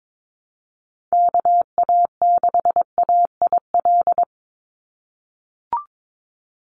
Morse code, heard as a keyed tone, about 22 wpm, 700 Hz. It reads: KA6AIL
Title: Call Signs - 22wpm